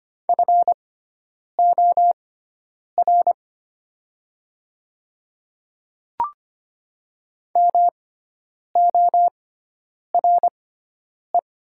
Morse code, heard as a keyed tone, about 25 wpm, spaced as for 9 wpm, 700 Hz. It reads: FOR MORE
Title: 6x Character Spacing - Top 300 Words - 25wpm